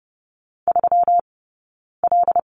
Morse code, heard as a keyed tone, about 30 wpm, 700 Hz. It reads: 3 L
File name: Lesson 20d - Number 3 - Two Characters 30wpm